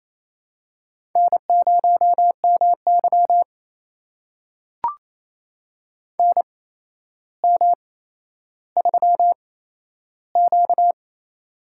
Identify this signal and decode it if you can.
Morse code, heard as a keyed tone, about 28 wpm, spaced as for 12 wpm, 700 Hz. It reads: N0MY NM3Q
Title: USA Call Signs - 4 Characters - 8x Character Spacing - 28wpm